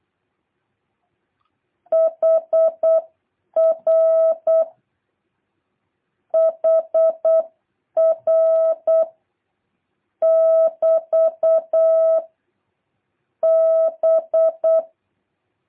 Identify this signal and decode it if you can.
Morse code, heard as a keyed tone, about 8 wpm, 650 Hz. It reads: HR HR <BT> B